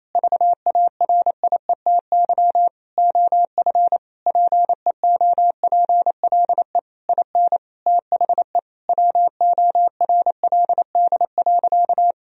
Morse code, heard as a keyed tone, about 28 wpm, 700 Hz. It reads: VARIETY OF PEOPLE IN THE WORLD.
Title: Sentences from Top 1000 Words - Review Part 4 - 28wpm